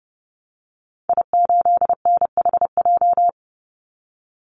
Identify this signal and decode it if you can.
Morse code, heard as a keyed tone, about 30 wpm, 700 Hz. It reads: I8NHJ
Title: Call Signs - 30wpm